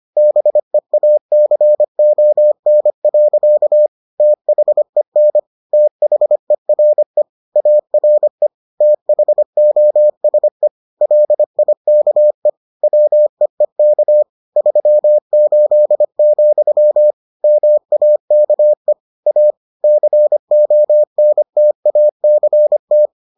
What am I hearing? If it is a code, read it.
Morse code, heard as a keyed tone, about 25 wpm, 600 Hz. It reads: BEACON. THEN THERE ARE THOSE LIKE WEEK 38, MAKE A CONTACT